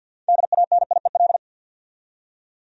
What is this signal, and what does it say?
Morse code, heard as a keyed tone, about 52 wpm, 700 Hz. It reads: DANIEL
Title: USA Common Names - 50wpm